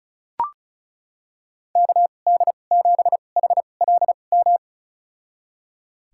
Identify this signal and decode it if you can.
Morse code, heard as a keyed tone, about 34 wpm, spaced as for 25 wpm, 700 Hz. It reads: KD7HLM
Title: USA Call Signs - 6 Characters - 2x Character Spacing - 35wpm